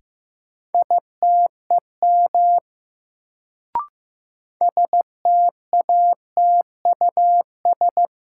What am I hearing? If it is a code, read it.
Morse code, heard as a keyed tone, about 15 wpm, 700 Hz. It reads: ITEM STATUS
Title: Sentences from Top 1400 Words - New Word Drill Alt - 15wpm